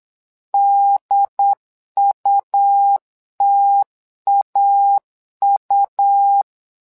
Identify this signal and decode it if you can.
Morse code, heard as a keyed tone, about 8 wpm, 800 Hz. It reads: DUTAU
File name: LEZIONI / 2 UHDN / 2 UHD 8wpm 2258